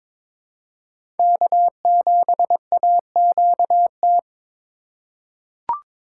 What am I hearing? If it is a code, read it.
Morse code, heard as a keyed tone, about 22 wpm, 700 Hz. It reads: K7AQT
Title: USA Call Signs - 5 Characters - 3x Character Spacing - 22wpm